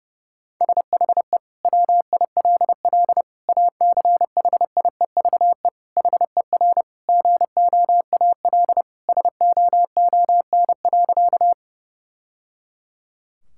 Morse code, heard as a keyed tone, about 32 wpm, 700 Hz. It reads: SHE WILL ACHIEVE HER GOAL SOON.
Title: Sentences from Top 1000 Words - Review Part 6 - 30wpm